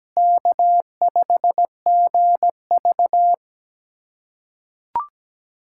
Morse code, heard as a keyed tone, about 17 wpm, 700 Hz. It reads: K5GV
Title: USA Callsign DE Callsign - 4 Characters - 17wpm